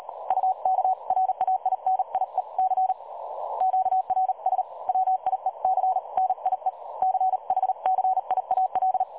Morse code, heard as a keyed tone, about 38 wpm, 750 Hz. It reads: UKRAINIEC QRS WIEC NIE CHCIAL
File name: dlazaawansowanych / nagraniazpasma / websdr recording 3565.0kHz